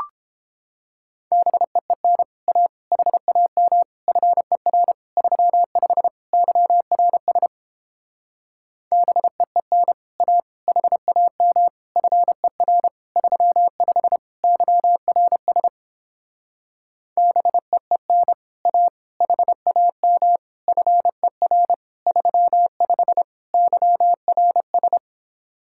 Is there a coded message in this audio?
Yes: Morse code from 1.3 to 25.0 s, heard as a keyed tone, about 29 wpm, 700 Hz. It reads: BEEN A HAM FER 35 YRS BEEN A HAM FER 35 YRS BEEN A HAM FER 35 YRS